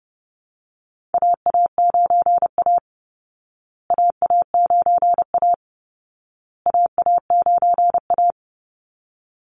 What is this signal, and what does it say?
Morse code, heard as a keyed tone, about 30 wpm, 700 Hz. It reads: AA9A AA9A AA9A
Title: Call Signs Repeated 2x - 30wpm